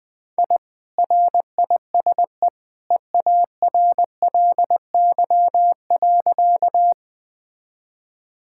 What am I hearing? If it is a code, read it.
Morse code, heard as a keyed tone, about 20 wpm, 700 Hz. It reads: I RISE EARLY.
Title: Sentences from Top 1000 Words - Review Part 6 - 20wpm